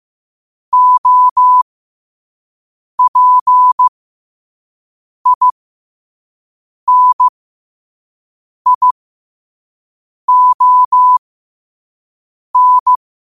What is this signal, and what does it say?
Morse code, heard as a keyed tone, about 15 wpm, spaced as for 5 wpm, 1000 Hz. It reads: OPINION
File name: qotdfiles / 5wpm